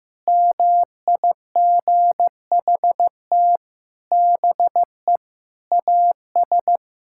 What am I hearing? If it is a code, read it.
Morse code, heard as a keyed tone, about 16 wpm, 700 Hz. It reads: MIGHT BE AS
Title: Sentences from Top 1600 Words 15wpm